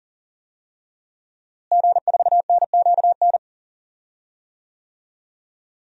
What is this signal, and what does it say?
Morse code, heard as a keyed tone, about 37 wpm, 700 Hz. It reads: G4NQN
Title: DXCC-100 Call Signs - 5 Characters - Repeated 3x - 40wpm